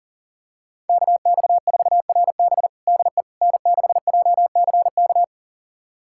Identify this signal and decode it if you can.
Morse code, heard as a keyed tone, about 37 wpm, 700 Hz. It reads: KX4RB DE N6JCK
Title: USA Callsign DE Callsign - 5 Characters - 40wpm